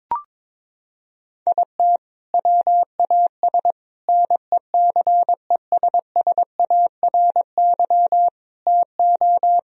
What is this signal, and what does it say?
Morse code, heard as a keyed tone, about 22 wpm, 700 Hz. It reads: IT WAS NECESSARY TO
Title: Sentences from Top 1000 Words - Review Part 4 - 22wpm